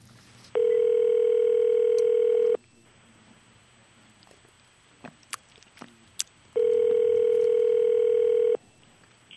Ringback tone (North America), 440 + 480 Hz.